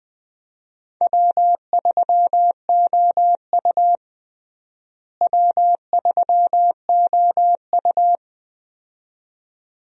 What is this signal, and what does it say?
Morse code, heard as a keyed tone, about 21 wpm, 700 Hz. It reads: W3OU W3OU
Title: USA Call Signs - 4 Characters - Repeated 2x - 20wpm